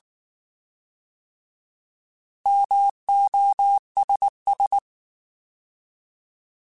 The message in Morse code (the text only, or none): MOSS